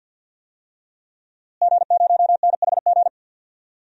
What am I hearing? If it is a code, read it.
Morse code, heard as a keyed tone, about 52 wpm, 700 Hz. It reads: G0NHG
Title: DXCC-100 Call Signs - 5 Characters - Repeated 2x - 50wpm